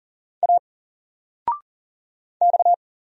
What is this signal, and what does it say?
Morse code, heard as a keyed tone, about 40 wpm, 700 Hz. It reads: A X
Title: Single Letter-Number 40wpm